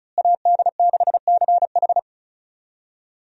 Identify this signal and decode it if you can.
Morse code, heard as a keyed tone, about 36 wpm, 700 Hz. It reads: AD6CH
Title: USA Call Signs - 5 Characters - Repeated 2x - 35wpm